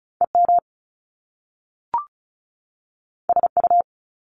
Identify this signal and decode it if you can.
Morse code, heard as a keyed tone, about 36 wpm, 700 Hz. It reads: EM SU
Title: Top Two Letter Combinations in Words 35wpm